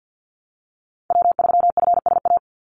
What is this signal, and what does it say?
Morse code, heard as a keyed tone, about 50 wpm, 700 Hz. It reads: W3FSA
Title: Call Signs Repeated 3x - 50wpm